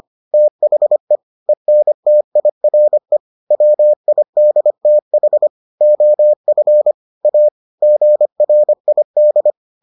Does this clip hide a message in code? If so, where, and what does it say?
Yes: Morse code from 0.3 to 9.5 s, heard as a keyed tone, about 25 wpm, 600 Hz. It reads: THE ENTIRE WIDTH OF A GRID